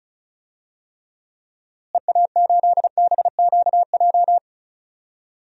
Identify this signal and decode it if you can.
Morse code, heard as a keyed tone, about 34 wpm, 700 Hz. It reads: EA8BQJ
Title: DXCC-100 Call Signs - 6 Characters - Repeated 2x - 35wpm